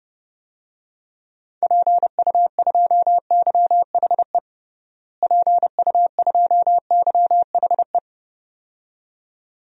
Morse code, heard as a keyed tone, about 32 wpm, 700 Hz. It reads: PU2YHE PU2YHE
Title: DXCC-100 Call Signs - 6 Characters - Repeated 2x - 30wpm